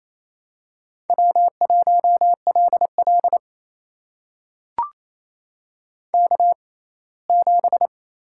Morse code, heard as a keyed tone, about 28 wpm, spaced as for 19 wpm, 700 Hz. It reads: W1LL K7
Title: USA Call Signs - 6x Character Spacing - 28wpm